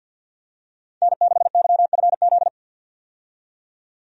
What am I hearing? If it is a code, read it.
Morse code, heard as a keyed tone, about 51 wpm, 700 Hz. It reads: N6YFZ
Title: USA Call Signs - 5 Characters - 50wpm